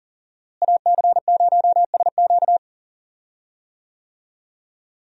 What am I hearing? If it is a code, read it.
Morse code, heard as a keyed tone, about 40 wpm, 690 Hz. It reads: AC0SQ
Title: USA Call Signs - 5 Characters - Repeated 3x - 40wpm